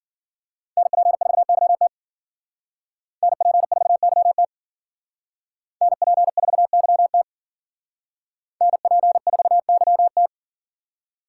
Morse code, heard as a keyed tone, about 51 wpm, 700 Hz. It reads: NP4YT NP4YT NP4YT NP4YT